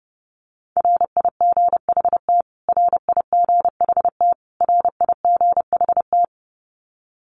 Morse code, heard as a keyed tone, about 30 wpm, 700 Hz. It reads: RIGHT RIGHT RIGHT
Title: Top 500 Words Repeated 30wpm